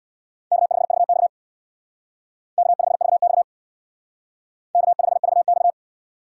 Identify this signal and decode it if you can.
Morse code, heard as a keyed tone, about 64 wpm, 700 Hz. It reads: K5VX K5VX K5VX